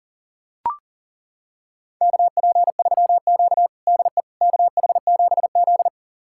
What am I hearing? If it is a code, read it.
Morse code, heard as a keyed tone, about 40 wpm, 700 Hz. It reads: KP3Q DE KH7Z
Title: USA Callsign DE Callsign - 4 Characters - 40wpm